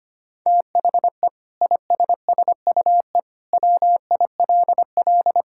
Morse code, heard as a keyed tone, about 25 wpm, 700 Hz. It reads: THE ISSUE WILL
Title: Sentences from Top 1000 Words - Review Part 6 - 25wpm